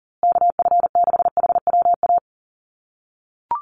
Morse code, heard as a keyed tone, about 40 wpm, 700 Hz. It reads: KF6HWA
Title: USA Call Signs Repeated 2x 40wpm